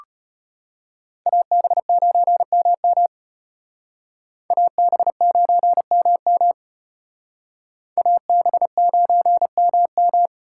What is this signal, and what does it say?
Morse code, heard as a keyed tone, about 32 wpm, 700 Hz. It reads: AB9MM AB9MM AB9MM